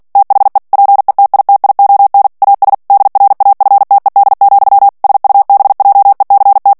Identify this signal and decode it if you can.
Morse code, heard as a keyed tone, about 48 wpm, 800 Hz. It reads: THE PETITION AS DRAFTED, SUBJECT
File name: ANGLAIS texte clair 3 vitesse 45 mots minute